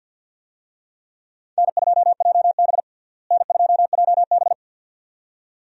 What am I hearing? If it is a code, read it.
Morse code, heard as a keyed tone, about 53 wpm, 700 Hz. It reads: N2JB N2JB
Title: USA Call Signs - 4 Characters - Repeated 2x - 50wpm